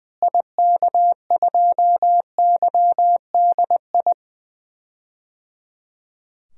Morse code, heard as a keyed tone, about 20 wpm, 700 Hz. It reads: IK2YDI
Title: DXCC-100 Call Signs - 6 Characters - Repeated 3x - 20wpm